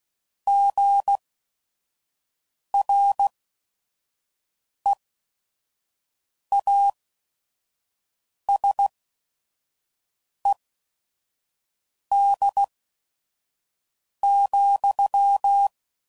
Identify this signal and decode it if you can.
Morse code, heard as a keyed tone, about 16 wpm, spaced as for 5 wpm, 800 Hz. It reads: GREASED,